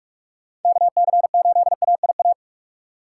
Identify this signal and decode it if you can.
Morse code, heard as a keyed tone, about 47 wpm, 700 Hz. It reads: KC8AIA
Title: USA Call Signs - 6 Characters - 6x Character Spacing - 45wpm